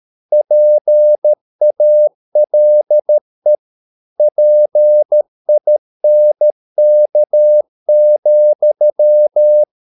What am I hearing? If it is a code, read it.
Morse code, heard as a keyed tone, about 13 wpm, 600 Hz. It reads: PALE PINK,